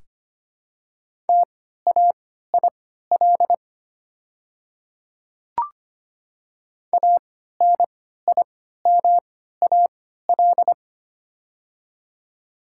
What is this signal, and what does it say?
Morse code, heard as a keyed tone, about 25 wpm, spaced as for 14 wpm, 690 Hz. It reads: TAIL ANIMAL
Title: 3x Character Spacing - Top 500 Words - 25wpm